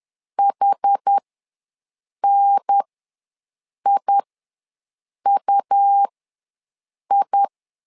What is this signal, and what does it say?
Morse code, heard as a keyed tone, about 10 wpm, spaced as for 6 wpm, 800 Hz. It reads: HNIUI
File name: LEZIONI / 4 Q ws 10wpm 2306